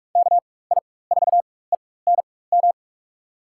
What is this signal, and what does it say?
Morse code, heard as a keyed tone, about 46 wpm, spaced as for 21 wpm, 700 Hz. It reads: KI4ENM
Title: USA Call Signs - 6 Characters - 4x Character Spacing - 45wpm